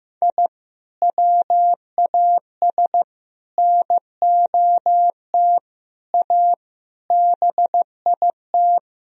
Morse code, heard as a keyed tone, about 15 wpm, 700 Hz. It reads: I WAS NOT A BIT
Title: Sentences from Top 1000 Words - Review Part 4 - 15wpm